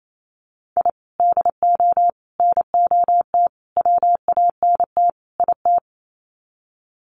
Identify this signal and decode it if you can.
Morse code, heard as a keyed tone, about 28 wpm, 700 Hz. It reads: I DO NOT WANT IT
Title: Sets of 5 Words Encore2 28wpm